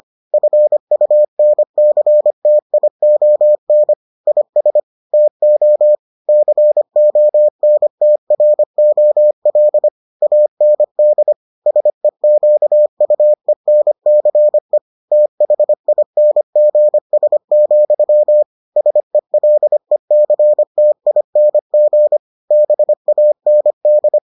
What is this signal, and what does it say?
Morse code, heard as a keyed tone, about 25 wpm, 600 Hz. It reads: FUNCTION IS TO CONTROL AND SEQUENCE THINGS, SELECTING BAND